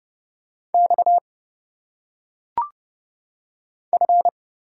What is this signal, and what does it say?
Morse code, heard as a keyed tone, about 30 wpm, 700 Hz. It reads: X F
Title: Single Letters - Phonetic Pronunciation 30wpm